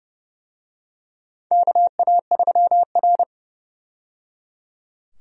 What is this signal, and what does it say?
Morse code, heard as a keyed tone, about 30 wpm, 700 Hz. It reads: KA3R
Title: USA Call Signs - 4 Characters - 30wpm